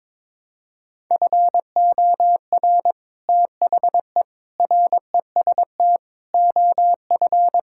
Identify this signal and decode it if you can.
Morse code, heard as a keyed tone, about 22 wpm, 700 Hz. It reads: FOR THE REST OF
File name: practice / 22wpm / Build A Phrase - 5-Word Phrases - 22wpm